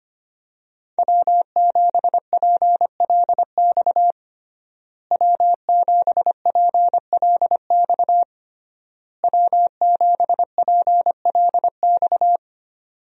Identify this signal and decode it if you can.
Morse code, heard as a keyed tone, about 25 wpm, 700 Hz. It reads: W7PLX W7PLX W7PLX
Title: USA Call Signs - 5 Characters - Repeated 3x - 25wpm